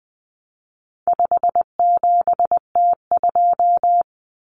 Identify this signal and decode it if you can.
Morse code, heard as a keyed tone, about 20 wpm, 700 Hz. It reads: 57T2
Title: Common Cut Numbers - 4-Digit 20wpm